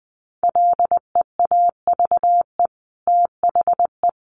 Morse code, heard as a keyed tone, about 20 wpm, 700 Hz. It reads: LEAVE THE
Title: Sentences from Top 1500 Words 20wpm